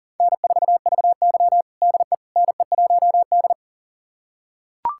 Morse code, heard as a keyed tone, about 40 wpm, 690 Hz. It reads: N4VY DE NE1D
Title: USA Callsign DE Callsign - 4 Characters - 40wpm